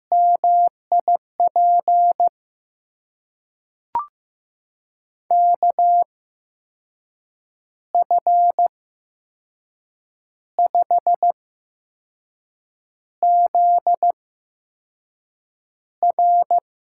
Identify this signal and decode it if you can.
Morse code, heard as a keyed tone, about 15 wpm, spaced as for 5 wpm, 700 Hz. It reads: MIP KF5ZR
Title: USA Call Signs - 8x Character Spacing - 15wpm